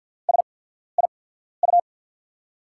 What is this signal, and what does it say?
Morse code, heard as a keyed tone, about 50 wpm, spaced as for 14 wpm, 700 Hz. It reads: SIU
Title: USA Call Signs - 5 Characters - 8x Character Spacing - 50wpm